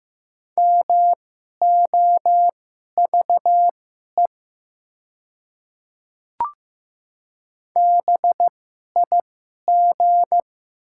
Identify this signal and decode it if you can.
Morse code, heard as a keyed tone, about 15 wpm, spaced as for 11 wpm, 700 Hz. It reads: MOVE BIG
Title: 2x Character Spacing - Top 200 Words - 15wpm